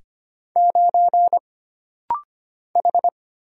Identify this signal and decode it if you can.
Morse code, heard as a keyed tone, about 25 wpm, 700 Hz. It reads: 9 H
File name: Single Character (Letter:Number:Punctuation:Prosign) - Rapid-Fire - 25wpm